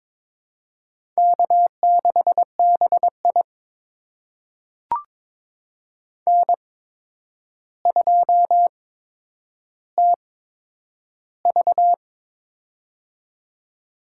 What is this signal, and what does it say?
Morse code, heard as a keyed tone, about 22 wpm, spaced as for 9 wpm, 700 Hz. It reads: K6BI N2TV
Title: USA Call Signs - 4 Characters - 8x Character Spacing - 22wpm